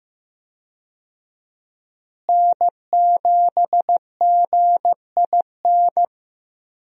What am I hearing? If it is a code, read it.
Morse code, heard as a keyed tone, about 15 wpm, 700 Hz. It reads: N7GIN
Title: USA Call Signs - 5 Characters - Repeated 3x - 15wpm